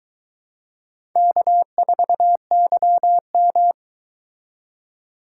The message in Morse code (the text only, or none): K4YM